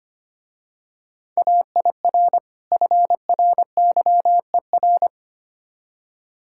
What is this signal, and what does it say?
Morse code, heard as a keyed tone, about 27 wpm, 690 Hz. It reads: AIR FRYER
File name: Easy Words - Speed-Racing with Hints - 25wpm